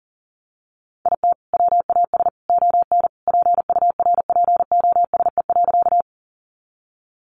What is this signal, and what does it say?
Morse code, heard as a keyed tone, about 40 wpm, 700 Hz. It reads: IT WAS ON PURPOSE.
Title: Sentences from Top 800 Words 40wpm